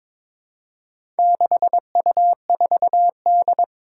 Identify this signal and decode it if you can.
Morse code, heard as a keyed tone, about 22 wpm, 700 Hz. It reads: 6U4D
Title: Four Challenging Letter-Number Combinations 22wpm